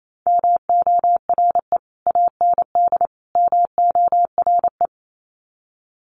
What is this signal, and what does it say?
Morse code, heard as a keyed tone, about 28 wpm, 700 Hz. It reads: MORE AND MORE
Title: Sets of 3 Words 28wpm